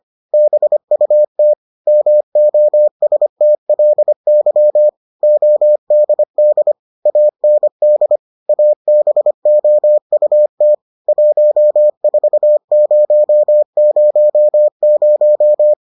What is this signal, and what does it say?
Morse code, heard as a keyed tone, about 25 wpm, 600 Hz. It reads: BUT MOSTLY ODD AND ABOUT 14000